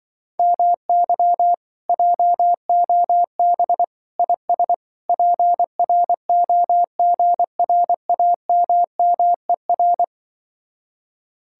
Morse code, heard as a keyed tone, about 25 wpm, 700 Hz. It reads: MY JOB IS PROGRAMMER
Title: Speed-Racing Common QSO Phrases - 20wpm